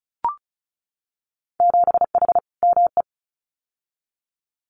Morse code, heard as a keyed tone, about 34 wpm, 700 Hz. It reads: 7H ME
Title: Winter Field Day Exchanges Encore 35wpm